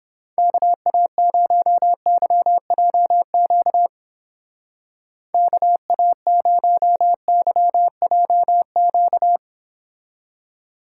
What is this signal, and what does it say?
Morse code, heard as a keyed tone, about 31 wpm, 700 Hz. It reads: KA0YJQ KA0YJQ